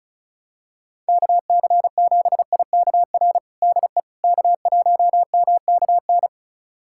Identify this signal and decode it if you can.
Morse code, heard as a keyed tone, about 36 wpm, 700 Hz. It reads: KC7IKR DE K1MKN
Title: USA Callsign DE Callsign 35wpm